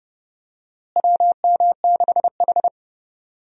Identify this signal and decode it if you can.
Morse code, heard as a keyed tone, about 30 wpm, 700 Hz. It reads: WM6H
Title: USA Call Signs - 4 Characters - Repeated 2x - 30wpm